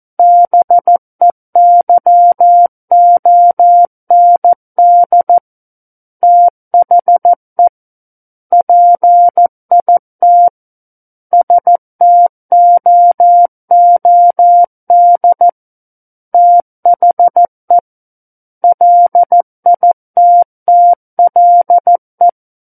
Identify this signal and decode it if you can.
Morse code, heard as a keyed tone, about 14 wpm, 700 Hz. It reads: BEYOND THE PIT STOOD THE LITTLE
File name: Book One / 05-Chapter 5 - 14 WPM